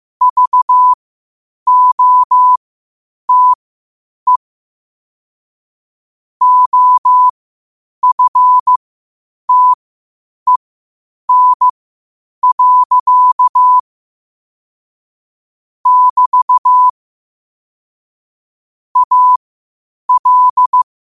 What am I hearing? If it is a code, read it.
Morse code, heard as a keyed tone, about 15 wpm, spaced as for 8 wpm, 1000 Hz. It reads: VOTE OFTEN. <BT> AL
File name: qotdfiles / 8wpm